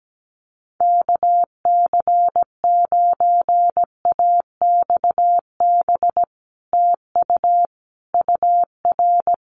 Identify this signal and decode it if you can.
Morse code, heard as a keyed tone, about 17 wpm, 700 Hz. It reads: KC9AXB TU UR
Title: Common QSO Phrases Encore 17wpm